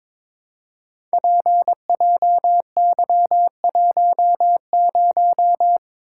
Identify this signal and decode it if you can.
Morse code, heard as a keyed tone, about 22 wpm, 700 Hz. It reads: PJY10
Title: Code Groups - 5 Letter:Numbers - 22wpm